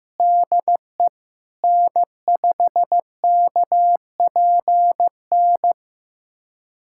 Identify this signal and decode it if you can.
Morse code, heard as a keyed tone, about 16 wpm, 700 Hz. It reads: DE N5KPN
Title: USA Callsign DE Callsign - 5 Characters - 15wpm